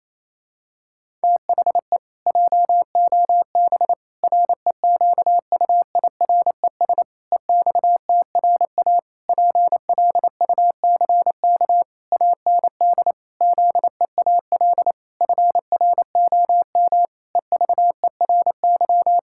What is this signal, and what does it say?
Morse code, heard as a keyed tone, about 28 wpm, 700 Hz. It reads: THE JOB REQUIRES EXTRA PLUCK AND ZEAL FROM EVERY
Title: Pangram Challenge v2 - 28wpm